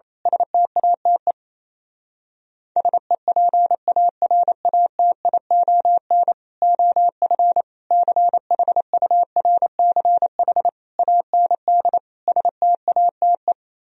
Morse code, heard as a keyed tone, about 27 wpm, 700 Hz. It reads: STATE SEPARATION OF CHURCH AND STATE
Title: Speed-Racing - Sets of 5 Words 28wpm